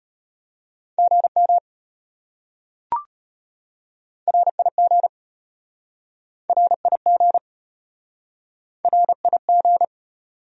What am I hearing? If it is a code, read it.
Morse code, heard as a keyed tone, about 33 wpm, 700 Hz. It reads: GM RIG RIG RIG